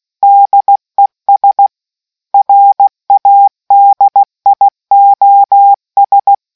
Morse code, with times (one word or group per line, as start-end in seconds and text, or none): 0.2-1.7 DES
2.3-6.3 RADIOS